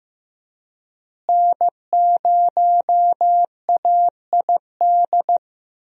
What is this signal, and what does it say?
Morse code, heard as a keyed tone, about 15 wpm, 700 Hz. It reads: N0AID
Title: USA Call Signs - 5 Characters - Repeated 2x - 15wpm